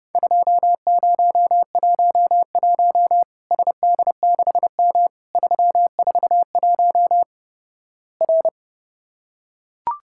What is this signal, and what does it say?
Morse code, heard as a keyed tone, about 32 wpm, 700 Hz. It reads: 2011 SD6M 341 R
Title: Worked All Europe DX Contest - QTC Exchange - 30wpm